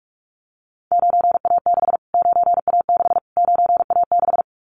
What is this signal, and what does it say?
Morse code, heard as a keyed tone, about 44 wpm, 700 Hz. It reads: 9A6 9A6 9A6
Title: Callsign Prefixes for Top 25 DXCC - Repeated 3x - 45wpm